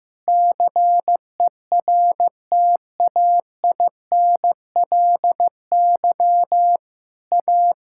Morse code, heard as a keyed tone, about 15 wpm, 700 Hz. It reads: CERTAINLY A
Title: Sentences from Top 1000 Words - Review Part 5 - 15wpm